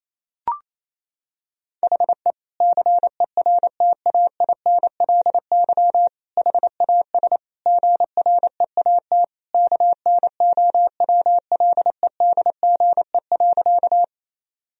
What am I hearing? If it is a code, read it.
Morse code, heard as a keyed tone, about 28 wpm, 690 Hz. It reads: HE CERTAINLY HAS GREAT KNOWLEDGE.
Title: Sentences from Top 1000 Words - Review Part 3 - 28wpm